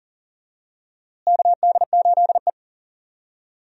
Morse code, heard as a keyed tone, about 39 wpm, 700 Hz. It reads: KD8E
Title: USA Call Signs - 4 Characters - Repeated 2x - 40wpm